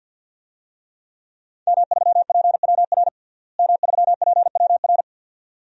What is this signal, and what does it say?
Morse code, heard as a keyed tone, about 47 wpm, 700 Hz. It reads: M3PWR M3PWR
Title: DXCC-100 Call Signs - 5 Characters - Repeated 2x - 50wpm